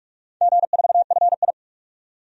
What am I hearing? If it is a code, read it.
Morse code, heard as a keyed tone, about 42 wpm, 700 Hz. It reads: G4FI
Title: DXCC-100 Call Signs - 4 Characters - 45wpm